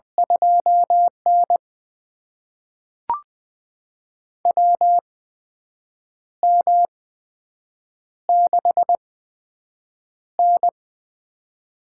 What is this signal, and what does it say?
Morse code, heard as a keyed tone, about 20 wpm, spaced as for 7 wpm, 700 Hz. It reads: 2N WM6N
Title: USA Call Signs - 4 Characters - 8x Character Spacing - 20wpm